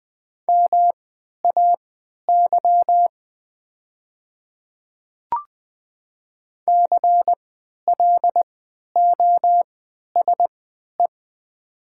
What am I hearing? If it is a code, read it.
Morse code, heard as a keyed tone, about 20 wpm, spaced as for 11 wpm, 700 Hz. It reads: MAY CLOSE
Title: 3x Character Spacing - Top 300 Words - 20wpm